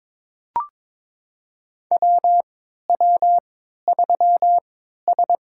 Morse code, heard as a keyed tone, about 22 wpm, spaced as for 13 wpm, 700 Hz. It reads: WW3S
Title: USA Call Signs - 4 Characters - 3x Character Spacing - 22wpm